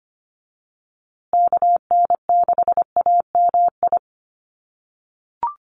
Morse code, heard as a keyed tone, about 25 wpm, 700 Hz. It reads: KN6AMI
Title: USA Call Signs Repeated 3x 25wpm